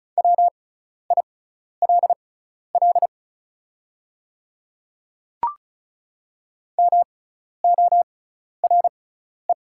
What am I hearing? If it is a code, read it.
Morse code, heard as a keyed tone, about 36 wpm, spaced as for 12 wpm, 700 Hz. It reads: WILL MORE
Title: 6x Character Spacing - Top 100 Words - 35wpm